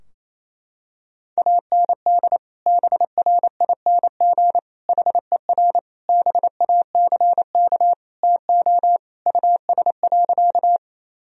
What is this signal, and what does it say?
Morse code, heard as a keyed tone, about 28 wpm, 700 Hz. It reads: AND BRING HER BACK TO US.
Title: Sentences from Top 1000 Words - Review Part 3 - 28wpm